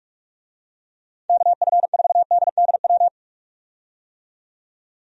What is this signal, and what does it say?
Morse code, heard as a keyed tone, about 46 wpm, 700 Hz. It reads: KF4DDW
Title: USA Call Signs - 6 Characters - 45wpm